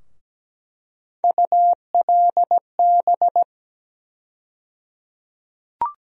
Morse code, heard as a keyed tone, about 17 wpm, 700 Hz. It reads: ULB
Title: USA Call Sign Postfixes - 3 Letters 17wpm